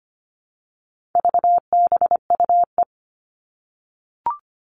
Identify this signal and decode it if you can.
Morse code, heard as a keyed tone, about 25 wpm, 700 Hz. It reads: VBUE
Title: All Cut Numbers - 4-Digit 25wpm